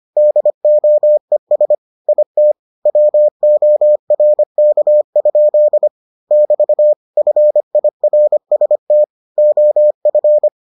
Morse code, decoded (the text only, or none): DOES IT WORK? <BT> FIRST OF